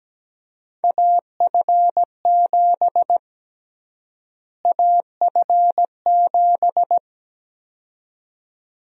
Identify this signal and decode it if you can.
Morse code, heard as a keyed tone, about 17 wpm, 700 Hz. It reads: AF7 AF7
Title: USA Call Sign Prefixes - Repeated 2x - 17wpm